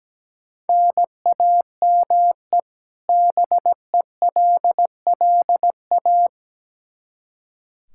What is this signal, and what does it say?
Morse code, heard as a keyed tone, about 17 wpm, 700 Hz. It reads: NAME BELLA
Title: Practice QSO - Exchange 1 - Beginning - 17wpm